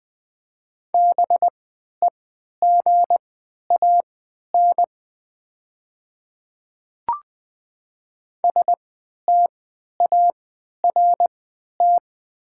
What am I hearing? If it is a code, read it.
Morse code, heard as a keyed tone, about 20 wpm, spaced as for 11 wpm, 700 Hz. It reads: BEGAN START
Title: 3x Character Spacing - Top 500 Words - 20wpm